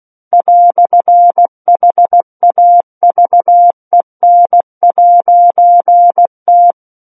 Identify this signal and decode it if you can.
Morse code, heard as a keyed tone, about 16 wpm, 700 Hz. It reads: "HAVEN'T
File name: Book One / 07-Chapter 7 - 16 WPM